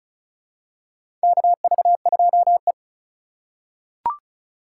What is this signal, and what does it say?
Morse code, heard as a keyed tone, about 34 wpm, 700 Hz. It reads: KV2E
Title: USA Call Signs - 4 Characters - 8x Character Spacing - 35wpm